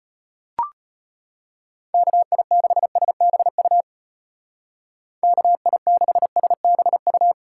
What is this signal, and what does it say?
Morse code, heard as a keyed tone, about 31 wpm, 700 Hz. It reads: KI6SBU KI6SBU
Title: Speed Racing - USA 2x3 Call Signs 25wpm